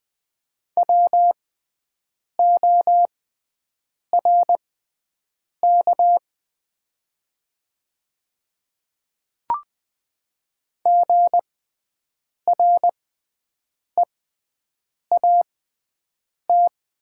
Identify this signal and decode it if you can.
Morse code, heard as a keyed tone, about 20 wpm, spaced as for 7 wpm, 700 Hz. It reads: WORK GREAT
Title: 6x Character Spacing - Top 200 Words - 20wpm